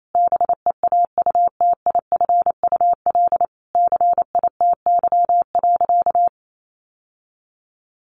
Morse code, heard as a keyed tone, about 28 wpm, 700 Hz. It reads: BEAUTIFUL CITY.
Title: Sentences from Top 1100 Words 28wpm